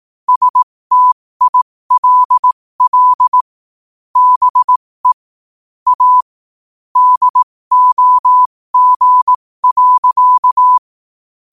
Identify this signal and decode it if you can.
Morse code, heard as a keyed tone, about 18 wpm, spaced as for 15 wpm, 1000 Hz. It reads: STILL BE A DOG.
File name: qotdfiles / 15wpm